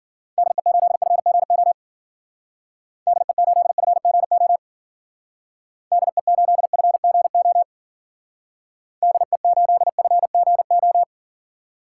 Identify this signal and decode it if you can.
Morse code, heard as a keyed tone, about 53 wpm, 700 Hz. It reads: DE8FGO DE8FGO DE8FGO DE8FGO